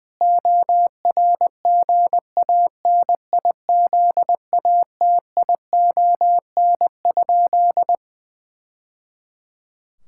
Morse code, heard as a keyed tone, about 20 wpm, 700 Hz. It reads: ORGANIZATION?
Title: Sentences from Top 1000 Words - Review Part 5 - 20wpm